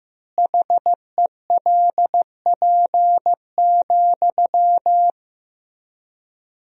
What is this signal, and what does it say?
Morse code, heard as a keyed tone, about 15 wpm, 700 Hz. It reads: HELP,
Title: 4x Word-Spacing - Sentences from Top 300 Words - 15wpm